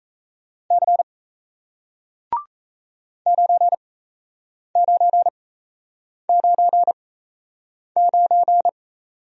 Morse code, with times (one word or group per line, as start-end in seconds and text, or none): 0.7-1.0 C
3.3-8.7 9999